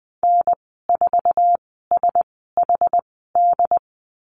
Morse code, heard as a keyed tone, about 20 wpm, spaced as for 15 wpm, 700 Hz. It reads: N4SHD
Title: 069 Lesson 12g - Letter d - Call Signs 20wpm